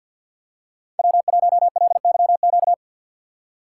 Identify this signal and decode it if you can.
Morse code, heard as a keyed tone, about 53 wpm, 700 Hz. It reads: W1LYQ
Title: USA Call Signs - 5 Characters - 2x Character Spacing - 50wpm